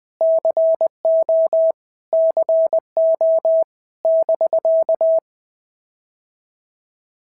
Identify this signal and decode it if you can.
Morse code, heard as a keyed tone, about 20 wpm, 650 Hz. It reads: CO CO <BK>
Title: POTA QSO Practice - 20wpm